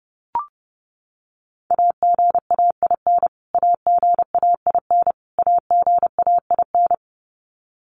Morse code, heard as a keyed tone, about 28 wpm, 700 Hz. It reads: AGAIN AGAIN AGAIN
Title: Top 300 Words Repeated 30wpm